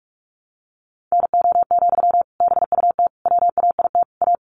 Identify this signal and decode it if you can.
Morse code, heard as a keyed tone, about 44 wpm, 700 Hz. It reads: NO, BUT WAIT A
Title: Sentences from Top 900 Words 45wpm